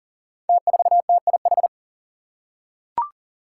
Morse code, heard as a keyed tone, about 39 wpm, 700 Hz. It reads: T4TIH
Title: Five Challenging Letter-Number Combinations 40wpm